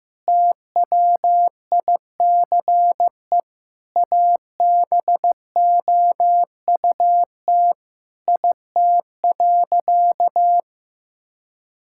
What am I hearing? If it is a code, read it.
Morse code, heard as a keyed tone, about 15 wpm, 700 Hz. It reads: TWICE ABOUT IT.
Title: Sentences from Top 1600 Words 15wpm